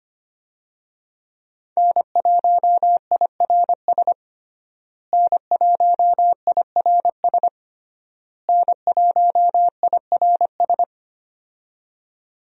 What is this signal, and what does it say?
Morse code, heard as a keyed tone, about 25 wpm, 700 Hz. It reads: N1IRS N1IRS N1IRS
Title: USA Call Signs - 5 Characters - Repeated 3x - 25wpm